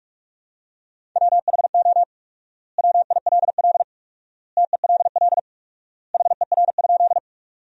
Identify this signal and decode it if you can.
Morse code, heard as a keyed tone, about 41 wpm, 700 Hz. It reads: WHO WILL TELL HER?
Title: 4x Word-Spacing - Sentences from Top 200 Words - 45wpm